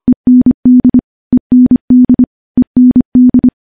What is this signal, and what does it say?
Morse code, heard as a keyed tone, about 25 wpm, 250 Hz. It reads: END END END
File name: morse-training-25WPM